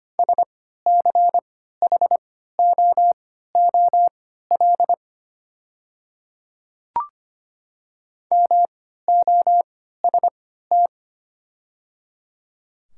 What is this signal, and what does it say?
Morse code, heard as a keyed tone, about 25 wpm, spaced as for 14 wpm, 700 Hz. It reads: SCHOOL MOST
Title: 3x Character Spacing - Top 300 Words - 25wpm